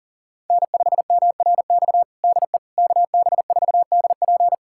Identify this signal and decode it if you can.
Morse code, heard as a keyed tone, about 40 wpm, 700 Hz. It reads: N5MRX DE KB4DP
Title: USA Callsign DE Callsign - 5 Characters - 40wpm